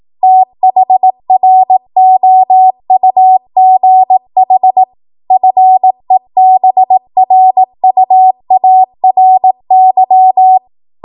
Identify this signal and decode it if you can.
Morse code, heard as a keyed tone, about 18 wpm, 750 Hz. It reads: THROUGH FEBRUARY
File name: Morse / 18WPM